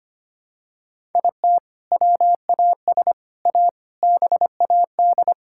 Morse code, heard as a keyed tone, about 25 wpm, 700 Hz. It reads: IT WAS A BAD
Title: Sentences from Top 1000 Words - Review Part 6 - 25wpm